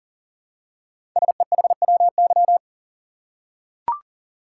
Morse code, heard as a keyed tone, about 42 wpm, 700 Hz. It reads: SEHWY
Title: Code Groups - 5 Letters - 40wpm